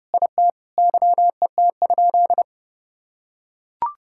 Morse code, heard as a keyed tone, about 30 wpm, 700 Hz. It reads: IT YET?
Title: Sentences from Top 1000 Words - Review Part 6 - 30wpm